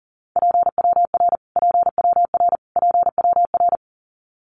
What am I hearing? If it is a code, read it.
Morse code, heard as a keyed tone, about 40 wpm, 700 Hz. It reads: PWR PWR PWR
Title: Abbreviations Repeated 3x - 40wpm